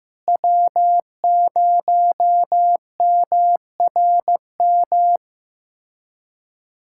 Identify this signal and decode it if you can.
Morse code, heard as a keyed tone, about 15 wpm, 700 Hz. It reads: W0MRM
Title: USA Callsign DE Callsign - 5 Characters - 15wpm